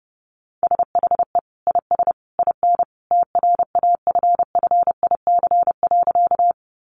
Morse code, heard as a keyed tone, about 30 wpm, 700 Hz. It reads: SHE IS IN TRAFFIC.
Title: Sentences from Top 1300 Words 30wpm